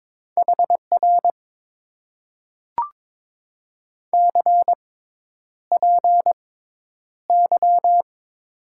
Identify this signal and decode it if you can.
Morse code, heard as a keyed tone, about 22 wpm, spaced as for 10 wpm, 700 Hz. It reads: HR CPY
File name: Top 90 QSO Elements - 6x Character Spacing - 22wpm